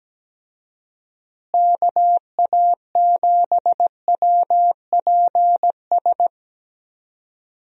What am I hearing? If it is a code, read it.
Morse code, heard as a keyed tone, about 17 wpm, 700 Hz. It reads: KA7WPS
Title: USA Call Signs - 6 Characters - Repeated 2x - 17wpm